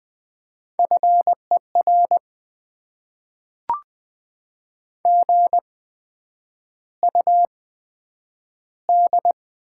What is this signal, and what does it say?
Morse code, heard as a keyed tone, about 20 wpm, spaced as for 9 wpm, 700 Hz. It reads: FER GUD
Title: Top 20 QSO Elements - 8x Character Spacing - 20wpm